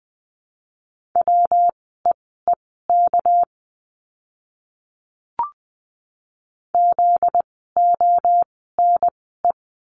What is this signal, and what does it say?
Morse code, heard as a keyed tone, about 20 wpm, spaced as for 14 wpm, 700 Hz. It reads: WEEK ZONE
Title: 244 Lesson 39f - Letter z - Words 20wpm